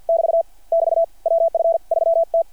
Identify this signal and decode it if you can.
Morse code, heard as a keyed tone, about 50 wpm, 660 Hz. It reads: <BT> <BT> WF 3T
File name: archived / CRSnet